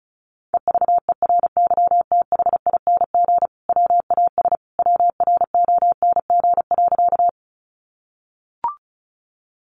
Morse code, heard as a keyed tone, about 34 wpm, 700 Hz. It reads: EVERYTHING WAS WRONG.
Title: Sentences from Top 900 Words 35wpm